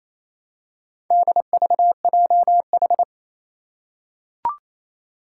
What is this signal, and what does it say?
Morse code, heard as a keyed tone, about 28 wpm, 700 Hz. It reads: DVJH
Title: Four Challenging Letter-Number Combinations 28wpm